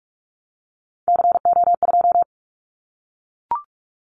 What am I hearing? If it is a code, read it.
Morse code, heard as a keyed tone, about 44 wpm, 700 Hz. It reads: CO2
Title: Callsign Prefixes for Top 100 DXCC - 45wpm